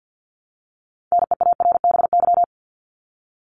Morse code, heard as a keyed tone, about 50 wpm, 700 Hz. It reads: NEARBY
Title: Top 1501-1750 Words 50wpm